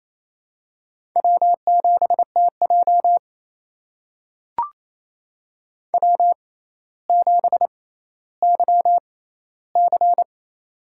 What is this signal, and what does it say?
Morse code, heard as a keyed tone, about 28 wpm, spaced as for 14 wpm, 700 Hz. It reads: W7TJ W7YC
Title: USA Call Signs - 4 Characters - 6x Character Spacing - 28wpm